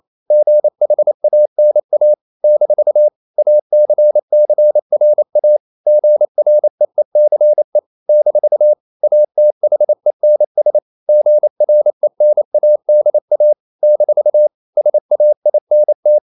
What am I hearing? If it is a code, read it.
Morse code, heard as a keyed tone, about 28 wpm, 600 Hz. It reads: GHANA - ACCRA GREECE - ATHENS GRENADA - SAINT
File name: CW / Practice / NationalCapitals28